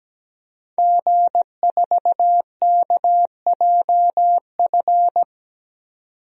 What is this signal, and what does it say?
Morse code, heard as a keyed tone, about 17 wpm, 700 Hz. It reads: G4KJF
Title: DXCC-100 Call Signs - 5 Characters - 17wpm